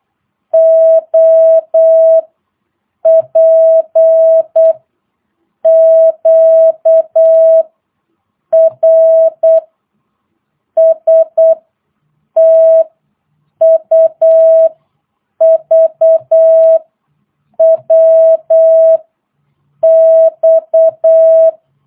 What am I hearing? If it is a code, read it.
Morse code, heard as a keyed tone, about 8 wpm, 660 Hz. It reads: O P Q R S T U V W X